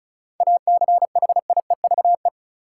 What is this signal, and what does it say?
Morse code, heard as a keyed tone, about 36 wpm, 700 Hz. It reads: ACHIEVE
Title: Sentences from Top 1000 Words - Review Part 5 - New Word Drill - 35wpm